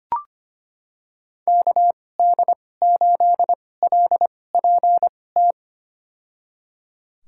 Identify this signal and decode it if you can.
Morse code, heard as a keyed tone, about 25 wpm, spaced as for 18 wpm, 700 Hz. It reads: KD8LPT
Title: USA Call Signs - 6 Characters - 2x Character Spacing - 25wpm